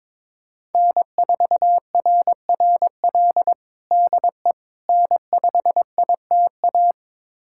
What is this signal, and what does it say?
Morse code, heard as a keyed tone, about 21 wpm, 700 Hz. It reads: N4RRL DE N5ITA
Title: USA Callsign DE Callsign - 5 Characters - 22wpm